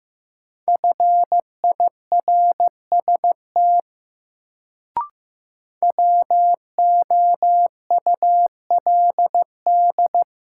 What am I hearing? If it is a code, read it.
Morse code, heard as a keyed tone, about 15 wpm, 700 Hz. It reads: FIRST WOULD
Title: Sentences from Top 100 Words - New Word Drill - 15wpm